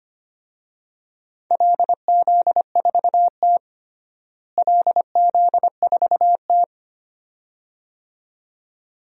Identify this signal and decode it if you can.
Morse code, heard as a keyed tone, about 25 wpm, 700 Hz. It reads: LZ4T LZ4T
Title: DXCC-100 Call Signs - 4 Characters - Repeated 2x - 25wpm